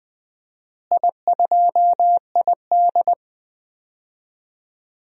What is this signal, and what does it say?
Morse code, heard as a keyed tone, about 20 wpm, 700 Hz. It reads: I2ID
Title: DXCC-100 Call Signs - 4 Characters - Repeated 2x - 20wpm